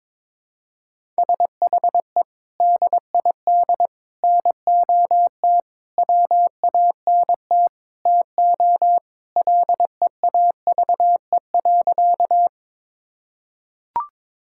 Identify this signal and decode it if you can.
Morse code, heard as a keyed tone, about 22 wpm, 700 Hz. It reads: SHE DID NOT WANT TO LEAVE.
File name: practice / 22wpm / Sentences with 4-Word Phrases - 22wpm